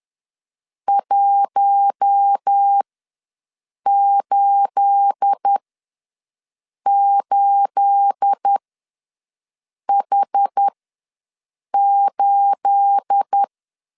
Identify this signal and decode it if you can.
Morse code, heard as a keyed tone, about 10 wpm, spaced as for 5 wpm, 800 Hz. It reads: 188H8
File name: LEZIONI / 9 28 ws 10wpm 4124